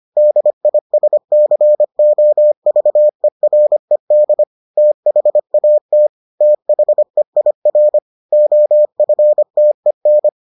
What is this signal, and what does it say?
Morse code, heard as a keyed tone, about 25 wpm, 600 Hz. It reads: DISCOVERED THAT THEIR OFTEN